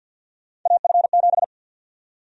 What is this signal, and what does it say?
Morse code, heard as a keyed tone, about 52 wpm, 700 Hz. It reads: AF7
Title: USA Call Sign Prefixes - 50wpm